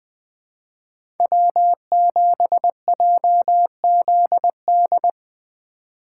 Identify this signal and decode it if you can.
Morse code, heard as a keyed tone, about 20 wpm, 700 Hz. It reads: W7JZD
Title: USA Call Signs - 5 Characters - 3x Character Spacing - 20wpm